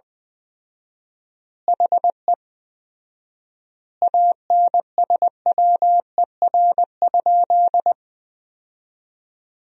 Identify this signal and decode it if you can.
Morse code, heard as a keyed tone, about 20 wpm, 700 Hz. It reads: HE ANSWER?
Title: 4x Word-Spacing - Sentences from Top 300 Words - 20wpm